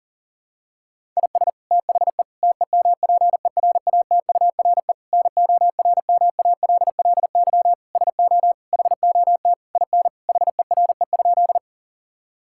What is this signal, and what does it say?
Morse code, heard as a keyed tone, about 37 wpm, 700 Hz. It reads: IS THE TEMPERATURE NORMALLY SO HOT IN HERE?
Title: Sentences from Top 1000 Words - Review Part 6 - 40wpm